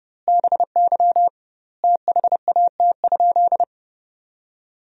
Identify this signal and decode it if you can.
Morse code, heard as a keyed tone, about 30 wpm, 700 Hz. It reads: BY THAT?
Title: 2x Word-Spacing - Sentences from Top 200 Words - 30wpm